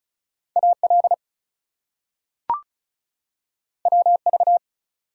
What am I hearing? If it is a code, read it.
Morse code, heard as a keyed tone, about 36 wpm, 700 Hz. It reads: AL WV
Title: US States and Canadian Provinces 35wpm